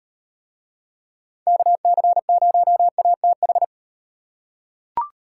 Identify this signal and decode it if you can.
Morse code, heard as a keyed tone, about 36 wpm, 700 Hz. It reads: KC0ATH